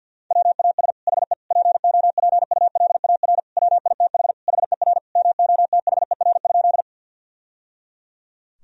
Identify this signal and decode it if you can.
Morse code, heard as a keyed tone, about 46 wpm, 700 Hz. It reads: WAS HE POPULAR WITH HER MOTHER?
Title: Sentences from Top 1000 Words - Review Part 3 - 50wpm